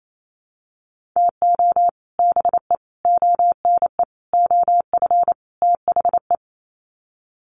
Morse code, heard as a keyed tone, about 28 wpm, 700 Hz. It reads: TO BE ONE OF THE
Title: Sets of 5 Words 28wpm